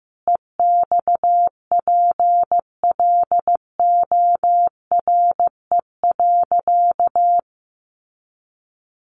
Morse code, heard as a keyed tone, about 14 wpm, 700 Hz. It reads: EXPLORE.
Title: Sentences from Top 1100 Words 15wpm